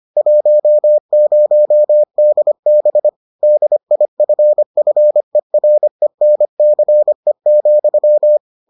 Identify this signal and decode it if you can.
Morse code, heard as a keyed tone, about 25 wpm, 600 Hz. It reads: 10DB DIFFERENCE,